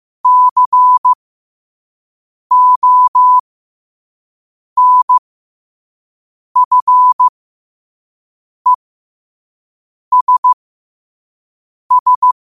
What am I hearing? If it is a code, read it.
Morse code, heard as a keyed tone, about 15 wpm, spaced as for 5 wpm, 1000 Hz. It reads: CONFESS
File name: qotdfiles / 5wpm